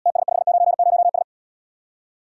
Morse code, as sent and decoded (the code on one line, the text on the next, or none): - ... ..... ----. ----- ...
TS590S